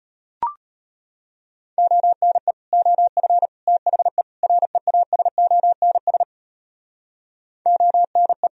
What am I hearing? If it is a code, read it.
Morse code, heard as a keyed tone, about 32 wpm, 700 Hz. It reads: ONE OF THE REASONS ONE